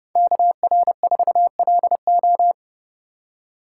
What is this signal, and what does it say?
Morse code, heard as a keyed tone, about 30 wpm, 700 Hz. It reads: KR4LO